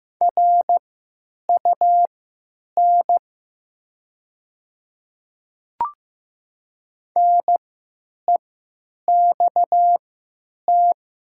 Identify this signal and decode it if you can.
Morse code, heard as a keyed tone, about 15 wpm, spaced as for 9 wpm, 700 Hz. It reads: RUN NEXT
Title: 3x Character Spacing - Top 300 Words - 15wpm